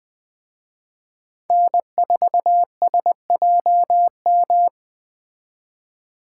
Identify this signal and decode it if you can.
Morse code, heard as a keyed tone, about 20 wpm, 700 Hz. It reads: N4SJM
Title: Speed Racing - USA Call Signs 17wpm